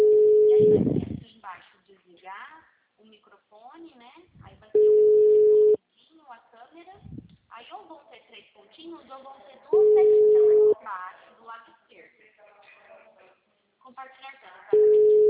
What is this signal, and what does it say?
Ringback tone (Europe, 425 Hz), with speech.
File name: out- -2001- - - .85707